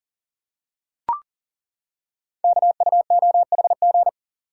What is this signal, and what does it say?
Morse code, heard as a keyed tone, about 42 wpm, 700 Hz. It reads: KUOHG
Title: Code Groups - 5 Letters - 40wpm